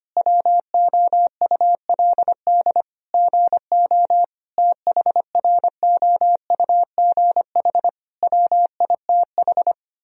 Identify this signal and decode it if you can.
Morse code, heard as a keyed tone, about 25 wpm, 700 Hz. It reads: WOULD GO THROUGH WITH
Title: Sentences from Top 1000 Words - Review Part 3 - 25wpm